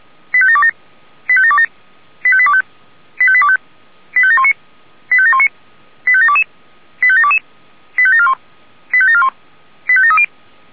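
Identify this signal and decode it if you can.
Five-tone selective call (ZVEI), heard as quick runs of five tones, tones of about 70 ms. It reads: 85727 85727 85825 85825 85719 85719 85722 85722 85741 85741 85730
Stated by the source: Five-tone ZVEI selective call received by SDR.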